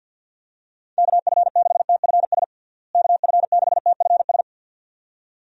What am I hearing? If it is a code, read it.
Morse code, heard as a keyed tone, about 52 wpm, 700 Hz. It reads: KF6TFS KF6TFS
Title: USA Call Signs - 6 Characters - Repeated 2x - 50wpm